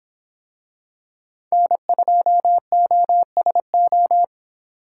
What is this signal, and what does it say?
Morse code, heard as a keyed tone, about 25 wpm, 700 Hz. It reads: N2OSO